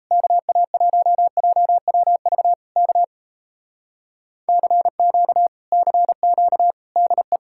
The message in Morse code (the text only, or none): KA1JWV K CQ CQ DE